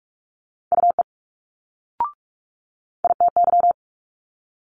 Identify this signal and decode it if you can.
Morse code, heard as a keyed tone, about 45 wpm, 700 Hz. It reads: UE ITY
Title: Multi-letter Phonemes Quick Fire 45wpm